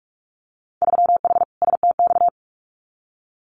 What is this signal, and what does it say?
Morse code, heard as a keyed tone, about 46 wpm, 700 Hz. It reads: 3H STX
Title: Winter Field Day Exchanges 45wpm